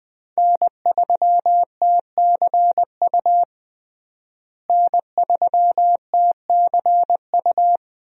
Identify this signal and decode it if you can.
Morse code, heard as a keyed tone, about 20 wpm, 700 Hz. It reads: N3TCU N3TCU
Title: USA Call Signs - 5 Characters - Repeated 2x - 20wpm